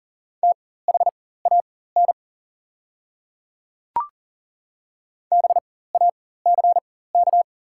Morse code, heard as a keyed tone, about 40 wpm, spaced as for 19 wpm, 700 Hz. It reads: THAN BACK
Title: 4x Character Spacing - Top 200 Words - 40wpm